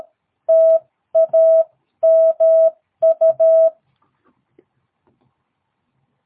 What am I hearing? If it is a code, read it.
Morse code, heard as a keyed tone, about 13 wpm, spaced as for 11 wpm, 650 Hz. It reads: TAMU
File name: jh8cbh / 2022cw